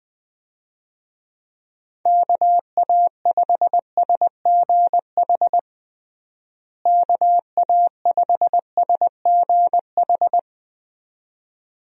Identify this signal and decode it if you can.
Morse code, heard as a keyed tone, about 21 wpm, 700 Hz. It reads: KA5SGH KA5SGH
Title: USA Call Signs - 6 Characters - Repeated 2x - 20wpm